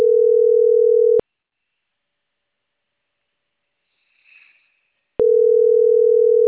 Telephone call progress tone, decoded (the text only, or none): ringback tone